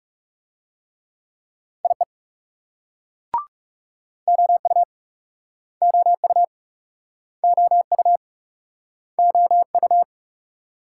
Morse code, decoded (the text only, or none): IE OU OU OU OU